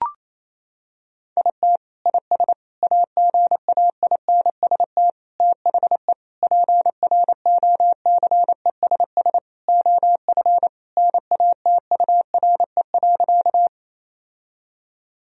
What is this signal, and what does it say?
Morse code, heard as a keyed tone, about 28 wpm, 700 Hz. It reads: IT IS AGAINST THE PROCESS OF NATURE.
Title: Sentences from Top 1000 Words - Review Part 3 - 28wpm